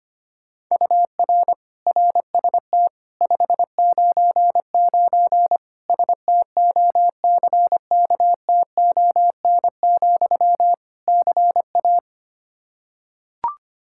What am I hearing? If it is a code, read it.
Morse code, heard as a keyed tone, about 25 wpm, 700 Hz. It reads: UR RST 599 STOCKTON, CA
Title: K1USN SST QSO Practice 25wpm